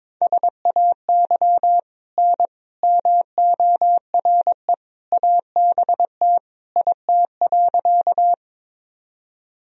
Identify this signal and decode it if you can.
Morse code, heard as a keyed tone, about 22 wpm, 700 Hz. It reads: SAY N MORE ABT IT.
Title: Sentences from Top 200 Words and Abbreviations 22wpm